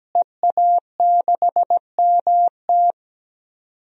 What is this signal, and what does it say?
Morse code, heard as a keyed tone, about 17 wpm, 700 Hz. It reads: EA6MT
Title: DXCC-100 Call Signs - 5 Characters - Repeated 2x - 17wpm